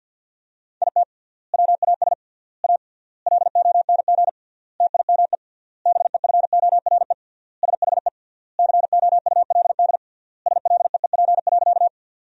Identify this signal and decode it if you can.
Morse code, heard as a keyed tone, about 50 wpm, 700 Hz. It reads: IT WAS A LONG TIME BEFORE SHE COULD SLEEP.
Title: 3x Word-Spacing - Sentences from Top 500 Words - 50wpm